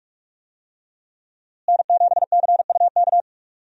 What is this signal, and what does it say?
Morse code, heard as a keyed tone, about 44 wpm, 700 Hz. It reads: N7CUK
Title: USA Call Signs - 5 Characters - Repeated 2x - 45wpm